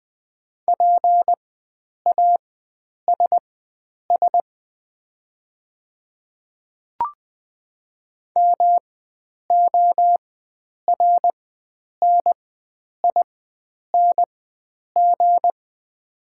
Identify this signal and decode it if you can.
Morse code, heard as a keyed tone, about 20 wpm, spaced as for 9 wpm, 690 Hz. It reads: PASS MORNING
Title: 4x Character Spacing - Top 500 Words - 20wpm